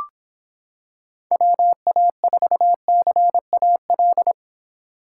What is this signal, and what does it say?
Morse code, heard as a keyed tone, about 26 wpm, 700 Hz. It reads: WA4CAL